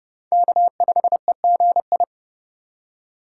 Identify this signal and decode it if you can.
Morse code, heard as a keyed tone, about 30 wpm, 700 Hz. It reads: K5EGI
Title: USA Call Signs - 5 Characters - 4x Character Spacing - 30wpm